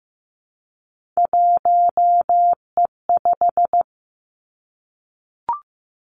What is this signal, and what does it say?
Morse code, heard as a keyed tone, about 15 wpm, 700 Hz. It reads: 1E5
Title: Three Challenging Letter-Number Combinations 15wpm